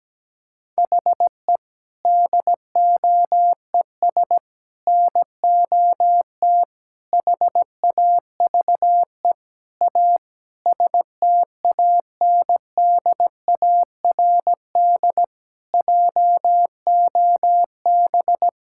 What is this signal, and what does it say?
Morse code, heard as a keyed tone, about 17 wpm, 700 Hz. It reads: HE DOES NOT HAVE A STANDARD JOB
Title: Sentences from Top 1000 Words - Review Part 6 - 17wpm